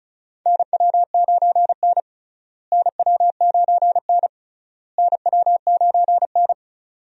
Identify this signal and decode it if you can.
Morse code, heard as a keyed tone, about 36 wpm, 700 Hz. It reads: NW9N NW9N NW9N
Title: USA Call Signs - 4 Characters - Repeated 3x - 35wpm